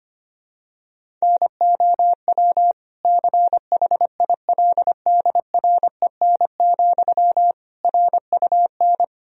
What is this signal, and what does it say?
Morse code, heard as a keyed tone, about 25 wpm, 700 Hz. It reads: NOW CHILDREN, RUN
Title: Sentences from Top 1000 Words - Review Part 1 - 25wpm